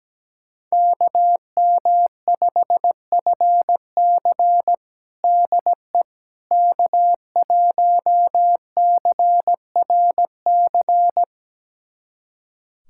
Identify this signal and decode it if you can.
Morse code, heard as a keyed tone, about 17 wpm, 700 Hz. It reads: KM5FC DE K1CRC
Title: USA Callsign DE Callsign - 5 Characters - 17wpm